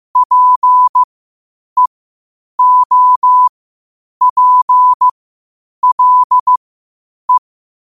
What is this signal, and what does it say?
Morse code, heard as a keyed tone, about 15 wpm, spaced as for 8 wpm, 1000 Hz. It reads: PEOPLE
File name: qotdfiles / 8wpm